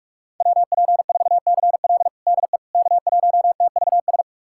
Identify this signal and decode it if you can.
Morse code, heard as a keyed tone, about 46 wpm, 700 Hz. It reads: WP4CL DE K1TVS
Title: USA Callsign DE Callsign - 5 Characters - 45wpm